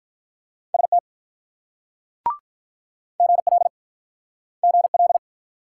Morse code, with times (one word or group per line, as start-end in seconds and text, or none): 0.7-1.0 ST
3.2-3.7 GL
4.6-5.2 GL